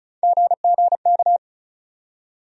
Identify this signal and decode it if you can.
Morse code, heard as a keyed tone, about 36 wpm, 700 Hz. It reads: GGK
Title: USA Call Sign Postfixes - 3 Letters 35wpm